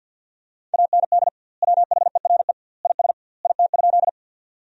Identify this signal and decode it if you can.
Morse code, heard as a keyed tone, about 50 wpm, 700 Hz. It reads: AND WHERE IS IT?
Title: 2x Word-Spacing - Sentences from Top 200 Words - 50wpm